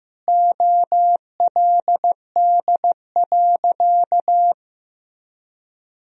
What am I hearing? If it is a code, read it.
Morse code, heard as a keyed tone, about 15 wpm, 700 Hz. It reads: OLD.
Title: Sentences from Top 1000 Words - Review Part 3 - 15wpm